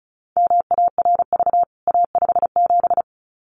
Morse code, heard as a keyed tone, about 34 wpm, 700 Hz. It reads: MARV A57
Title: CWT Exchange 35wpm